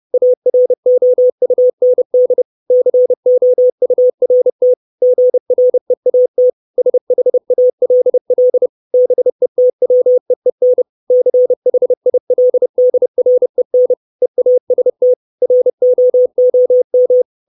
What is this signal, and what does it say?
Morse code, heard as a keyed tone, about 30 wpm, 500 Hz. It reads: AROUND COURT GREAT SHALL BETWEEN CHILDREN EAST ROOM
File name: 632 Words 30WPM